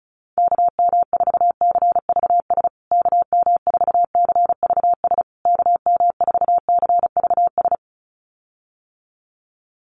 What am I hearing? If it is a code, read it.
Morse code, heard as a keyed tone, about 34 wpm, 700 Hz. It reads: KM4CVS KM4CVS KM4CVS
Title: USA Call Signs Repeated 3x 35wpm